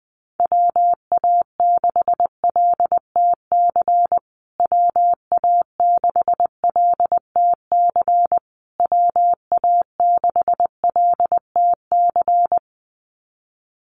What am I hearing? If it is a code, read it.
Morse code, heard as a keyed tone, about 20 wpm, 700 Hz. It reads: WA6LTC WA6LTC WA6LTC
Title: USA Call Signs Repeated 3x 20wpm